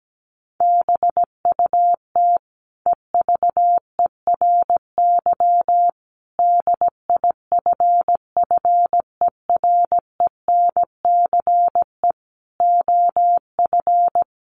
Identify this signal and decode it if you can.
Morse code, heard as a keyed tone, about 17 wpm, 700 Hz. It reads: BUT EVERY DIFFERENCE OF
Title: Sentences from Top 1400 Words 17wpm